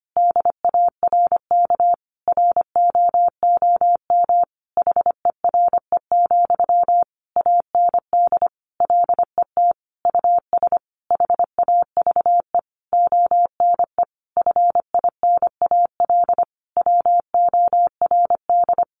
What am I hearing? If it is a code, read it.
Morse code, heard as a keyed tone, about 25 wpm, 700 Hz. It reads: DARK ROOM HERE, AND LET US HAVE ONE FINAL WORD
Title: Sentences from Top 1000 Words 25wpm